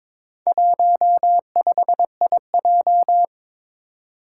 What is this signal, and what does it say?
Morse code, heard as a keyed tone, about 22 wpm, 700 Hz. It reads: 15IJ
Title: Four Challenging Letter-Number Combinations 22wpm